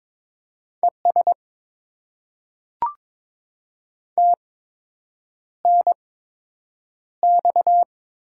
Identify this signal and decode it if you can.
Morse code, heard as a keyed tone, about 22 wpm, spaced as for 8 wpm, 700 Hz. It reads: ES TNX
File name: Top 20 QSO Elements - 8x Character Spacing - 22wpm